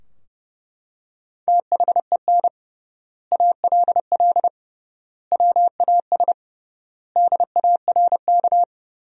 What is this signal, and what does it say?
Morse code, heard as a keyed tone, about 30 wpm, 700 Hz. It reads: THEN ALL WAS DARK
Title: 3x Word-Spacing - Sentences from Top 500 Words - 30wpm